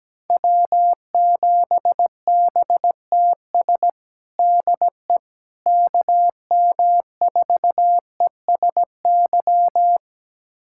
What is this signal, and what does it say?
Morse code, heard as a keyed tone, about 17 wpm, 700 Hz. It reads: W7BTS DE KM4ESY
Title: USA Callsign DE Callsign 17wpm